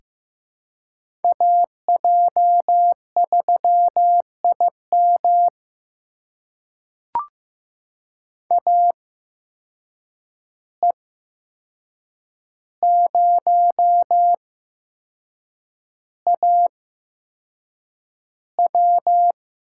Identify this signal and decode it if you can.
Morse code, heard as a keyed tone, about 15 wpm, spaced as for 6 wpm, 700 Hz. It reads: AJ3IM AE0AW
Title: USA Call Signs - 5 Characters - 8x Character Spacing - 15wpm